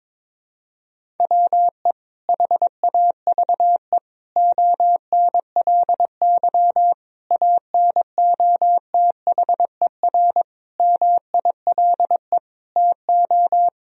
Morse code, heard as a keyed tone, about 21 wpm, 700 Hz. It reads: WE HAVE ONLY ANOTHER MILE TO
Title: Sentences from Top 1000 Words - Review Part 1 - 22wpm